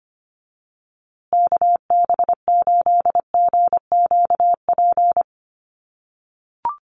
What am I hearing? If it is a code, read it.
Morse code, heard as a keyed tone, about 25 wpm, 700 Hz. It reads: KB8GQP
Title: USA Call Signs Repeated 3x 25wpm